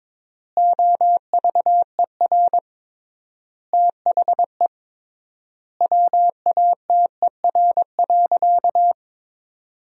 Morse code, heard as a keyed tone, about 22 wpm, 700 Hz. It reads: OVER THE WATER.
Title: 3x Word-Spacing - Sentences from Top 300 Words - 22wpm